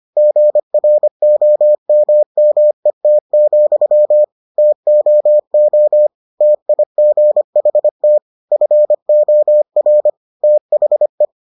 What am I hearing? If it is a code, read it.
Morse code, heard as a keyed tone, about 25 wpm, 600 Hz. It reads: GROMMET, TOO TIGHT FOR THE